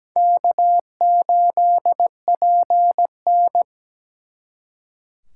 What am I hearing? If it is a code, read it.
Morse code, heard as a keyed tone, about 20 wpm, 700 Hz. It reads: K8PN
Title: Speed Racing - USA 1x2 Call Signs 17wpm